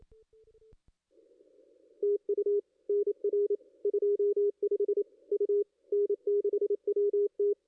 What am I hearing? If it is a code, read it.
Morse code, heard as a keyed tone, about 28 wpm, 410 Hz. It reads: TU NR 25 U N6WT
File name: contests / ss-cw-k5tr-op-n5rz -2200